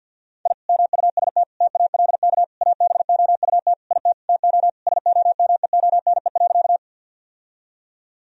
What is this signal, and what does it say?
Morse code, heard as a keyed tone, about 50 wpm, 700 Hz. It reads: I MUST TALK ABOUT IT TO SOMEONE.
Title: Sentences from Top 1000 Words - Review Part 4 - 50wpm